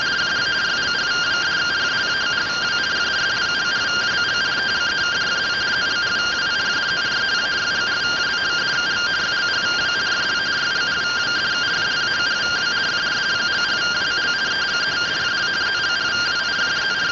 Two-tone teleprinter signal, about 170 Hz shift.